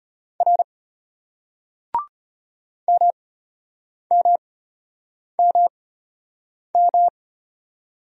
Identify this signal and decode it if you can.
Morse code, heard as a keyed tone, about 34 wpm, spaced as for 8 wpm, 700 Hz. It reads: R MMMM